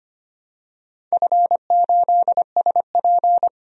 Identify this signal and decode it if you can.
Morse code, heard as a keyed tone, about 25 wpm, 700 Hz. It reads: F8SP
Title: DXCC-100 Call Signs - 4 Characters - 25wpm